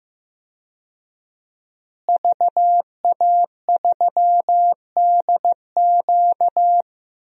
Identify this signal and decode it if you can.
Morse code, heard as a keyed tone, about 15 wpm, 700 Hz. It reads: VA3DQ
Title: DXCC-100 Call Signs - 5 Characters - Repeated 3x - 15wpm